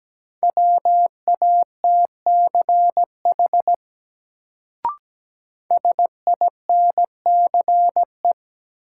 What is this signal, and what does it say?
Morse code, heard as a keyed tone, about 17 wpm, 700 Hz. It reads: WATCH SINCE
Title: Sentences from Top 300 Words - New Word Drill - 17wpm